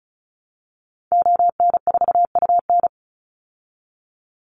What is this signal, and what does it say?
Morse code, heard as a keyed tone, about 34 wpm, 700 Hz. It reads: ON4UN
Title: Call Signs - 35wpm